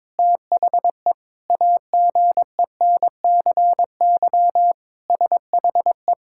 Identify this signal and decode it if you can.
Morse code, heard as a keyed tone, about 22 wpm, 700 Hz. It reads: THE AGENCY SHE
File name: Sentences with 7-Word Phrases - 22wpm